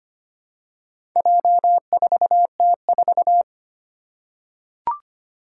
Morse code, heard as a keyed tone, about 25 wpm, 700 Hz. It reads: J4T4
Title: Four Challenging Letter-Number Combinations 25wpm